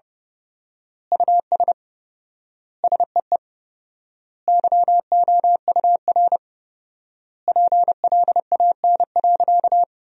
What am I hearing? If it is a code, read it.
Morse code, heard as a keyed tone, about 30 wpm, 700 Hz. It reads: US SEE YOUR PLAN.
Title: 4x Word-Spacing - Sentences from Top 500 Words - 30wpm